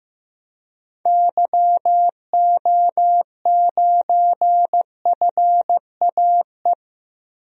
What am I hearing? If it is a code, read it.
Morse code, heard as a keyed tone, about 15 wpm, 700 Hz. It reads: YO9FAE
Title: DXCC-100 Call Signs - 6 Characters - 15wpm